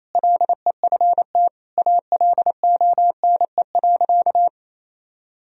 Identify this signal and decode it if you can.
Morse code, heard as a keyed tone, about 28 wpm, 700 Hz. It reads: LEFT ALONE.
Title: Sentences from Top 1000 Words - Review Part 4 - 28wpm